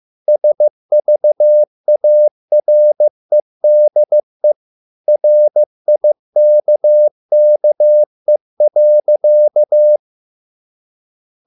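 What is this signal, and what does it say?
Morse code, heard as a keyed tone, about 15 wpm, 600 Hz. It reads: SVAREDE RIKKE.
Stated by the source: Morse code at a slow 15 wpm from CW med Gnister.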